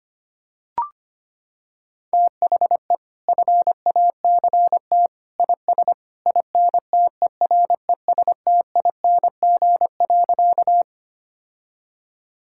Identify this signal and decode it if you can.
Morse code, heard as a keyed tone, about 25 wpm, 700 Hz. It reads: THE FACT IS INTERESTING.
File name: Sentences with 2-Word Phrases - 25wpm